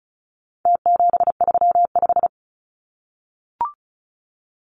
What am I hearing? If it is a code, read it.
Morse code, heard as a keyed tone, about 34 wpm, 700 Hz. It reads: T735
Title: Numbers - Contest Style - 4-Digit 35wpm